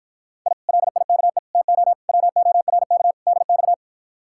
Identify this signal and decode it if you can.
Morse code, heard as a keyed tone, about 52 wpm, 700 Hz. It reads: I LIKE TO WORK DX